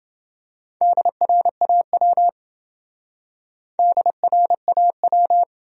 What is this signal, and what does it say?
Morse code, heard as a keyed tone, about 28 wpm, 700 Hz. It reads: DRAW DRAW